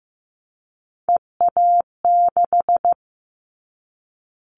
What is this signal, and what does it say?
Morse code, heard as a keyed tone, about 15 wpm, 700 Hz. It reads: EA6
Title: Callsign Prefixes for Top 50 DXCC - 15wpm